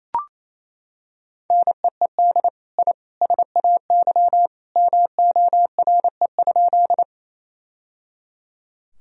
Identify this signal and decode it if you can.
Morse code, heard as a keyed tone, about 28 wpm, 700 Hz. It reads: NEED I SAY MORE?
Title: Sentences from Top 1000 Words - Review Part 1 - 28wpm